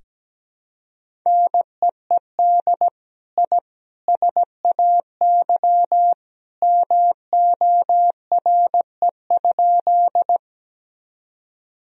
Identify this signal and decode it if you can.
Morse code, heard as a keyed tone, about 17 wpm, 700 Hz. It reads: NEED I SAY MORE?
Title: Sentences from Top 1000 Words - Review Part 1 - 17wpm